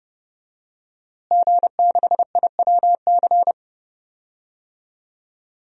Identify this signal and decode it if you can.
Morse code, heard as a keyed tone, about 30 wpm, 700 Hz. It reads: G6IWC
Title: DXCC-100 Call Signs - 5 Characters - Repeated 3x - 30wpm